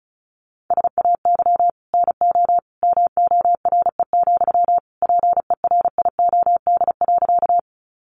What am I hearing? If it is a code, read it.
Morse code, heard as a keyed tone, about 34 wpm, 700 Hz. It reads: SAY NO MORE, PERIOD.
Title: Sentences from Top 800 Words 35wpm